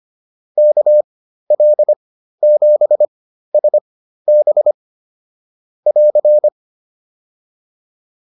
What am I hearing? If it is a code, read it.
Morse code, heard as a keyed tone, about 25 wpm, spaced as for 13 wpm, 600 Hz. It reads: KL7SB <AR>